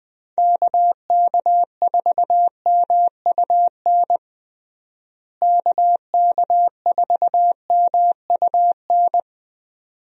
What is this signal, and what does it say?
Morse code, heard as a keyed tone, about 20 wpm, 700 Hz. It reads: KK4MUN KK4MUN
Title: USA Call Signs - 6 Characters - Repeated 2x - 20wpm